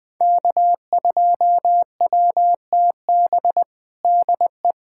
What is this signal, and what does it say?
Morse code, heard as a keyed tone, about 20 wpm, 700 Hz. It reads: K2WTB DE
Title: USA Callsign DE Callsign - 5 Characters - 20wpm